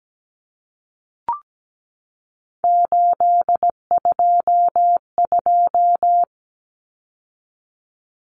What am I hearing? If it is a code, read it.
Morse code, heard as a keyed tone, about 17 wpm, 700 Hz. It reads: 822
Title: Numbers - 3-Digit 17wpm